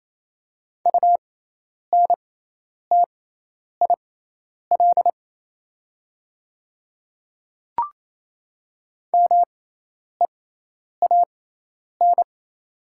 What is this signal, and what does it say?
Morse code, heard as a keyed tone, about 28 wpm, spaced as for 10 wpm, 700 Hz. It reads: UNTIL MEAN
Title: 6x Character Spacing - Top 300 Words - 28wpm